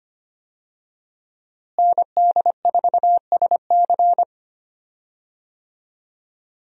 Morse code, heard as a keyed tone, about 25 wpm, 700 Hz. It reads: ND4SC
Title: USA Call Signs - 5 Characters - Repeated 2x - 25wpm